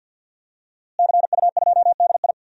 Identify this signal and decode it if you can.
Morse code, heard as a keyed tone, about 53 wpm, 700 Hz. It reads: CU2DI
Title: DXCC-100 Call Signs - 5 Characters - Repeated 2x - 50wpm